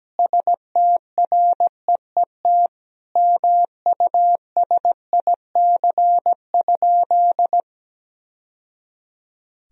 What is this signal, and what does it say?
Morse code, heard as a keyed tone, about 17 wpm, 700 Hz. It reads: STREET MUSIC?
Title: Sentences from Top 1000 Words - Review Part 3 - 17wpm